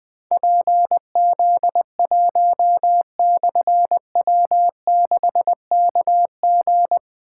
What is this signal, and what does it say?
Morse code, heard as a keyed tone, about 20 wpm, 700 Hz. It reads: PZ1/W6KG
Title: Difficult Call Signs 20wpm